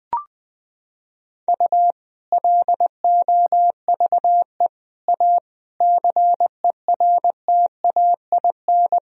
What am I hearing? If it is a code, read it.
Morse code, heard as a keyed tone, about 20 wpm, 700 Hz. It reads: U LOVE A CERTAIN
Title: Sentences from Top 500 Words and Abbreviations 20wpm